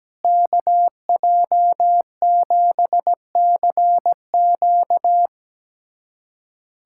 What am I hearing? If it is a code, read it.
Morse code, heard as a keyed tone, about 17 wpm, 700 Hz. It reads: KJ7CQ
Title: USA Callsign DE Callsign - 5 Characters - 17wpm